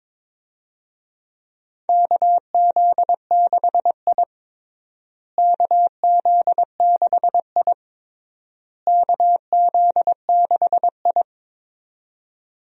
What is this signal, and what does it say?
Morse code, heard as a keyed tone, about 21 wpm, 700 Hz. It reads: KZ6I KZ6I KZ6I
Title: USA Call Signs - 4 Characters - Repeated 3x - 22wpm